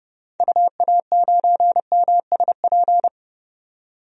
Morse code, heard as a keyed tone, about 32 wpm, 700 Hz. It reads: UA9MSP
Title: DXCC-100 Call Signs - 6 Characters - 30wpm